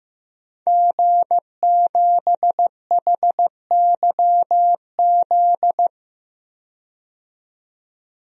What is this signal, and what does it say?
Morse code, heard as a keyed tone, about 15 wpm, 700 Hz. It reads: G7HYZ
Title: DXCC-100 Call Signs - 5 Characters - 15wpm